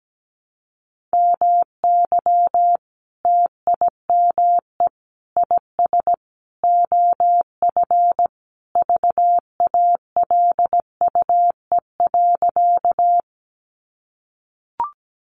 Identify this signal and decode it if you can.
Morse code, heard as a keyed tone, about 17 wpm, 700 Hz. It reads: MY TIME IS OF VALUE.
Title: Sentences from Top 600 Words 17wpm